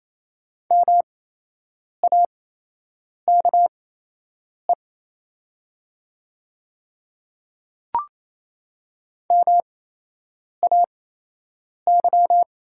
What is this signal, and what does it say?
Morse code, heard as a keyed tone, about 28 wpm, spaced as for 8 wpm, 700 Hz. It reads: MAKE MAY
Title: 8x Character Spacing - Top 100 Words - 28wpm